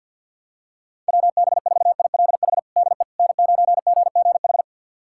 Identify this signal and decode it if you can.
Morse code, heard as a keyed tone, about 46 wpm, 700 Hz. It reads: WB4ILH DE N9GGH
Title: USA Callsign DE Callsign 50wpm